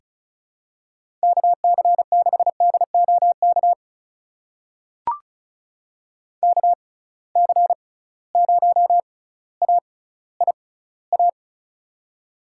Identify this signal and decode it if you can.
Morse code, heard as a keyed tone, about 36 wpm, spaced as for 18 wpm, 690 Hz. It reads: KC6DOK KC0AIA
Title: USA Call Signs - 6 Characters - 6x Character Spacing - 35wpm